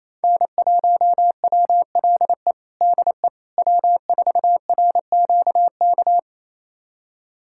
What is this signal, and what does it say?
Morse code, heard as a keyed tone, about 28 wpm, 700 Hz. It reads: N1WLE DE W4RQK
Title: USA Callsign DE Callsign - 5 Characters - 28wpm